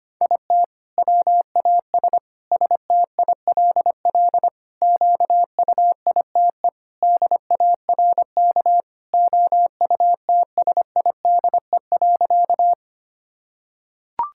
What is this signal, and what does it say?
Morse code, heard as a keyed tone, about 25 wpm, 700 Hz. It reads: IT WAS STILL QUITE DARK OUTSIDE.
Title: Sentences from Top 1000 Words - Review Part 6 - 25wpm